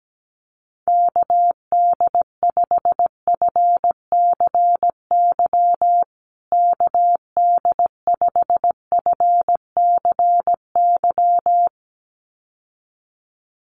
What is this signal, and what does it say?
Morse code, heard as a keyed tone, about 17 wpm, 700 Hz. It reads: KD5FCY KD5FCY
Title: USA Call Signs Repeated 2x 17wpm